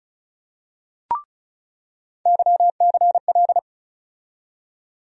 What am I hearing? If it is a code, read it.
Morse code, heard as a keyed tone, about 36 wpm, 700 Hz. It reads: YCL
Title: USA Call Sign Postfixes - 3 Letters 35wpm